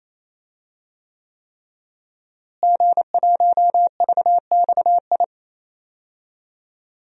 Morse code, heard as a keyed tone, about 28 wpm, 700 Hz. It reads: G1VXI
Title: DXCC-100 Call Signs - 5 Characters - Repeated 3x - 28wpm